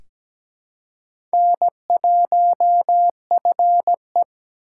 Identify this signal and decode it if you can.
Morse code, heard as a keyed tone, about 17 wpm, 700 Hz. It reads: N1FE
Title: K1USN SST QSO Practice - 17wpm